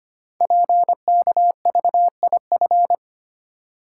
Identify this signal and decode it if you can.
Morse code, heard as a keyed tone, about 25 wpm, 700 Hz. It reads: PKVIF
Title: Code Groups - 5 Letters - 25wpm